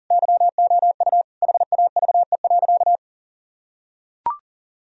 Morse code, heard as a keyed tone, about 40 wpm, 700 Hz. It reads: YOU HAVE.
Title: Sentences from Top 1000 Words - Review Part 3 - 40wpm